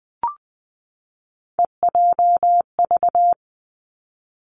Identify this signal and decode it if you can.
Morse code, heard as a keyed tone, about 19 wpm, 700 Hz. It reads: EJV
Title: Three Challenging Letter-Number Combinations 20wpm